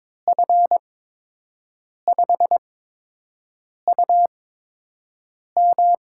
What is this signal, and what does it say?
Morse code, heard as a keyed tone, about 22 wpm, spaced as for 6 wpm, 700 Hz. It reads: F5UM
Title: USA Call Signs - 6 Characters - 8x Character Spacing - 22wpm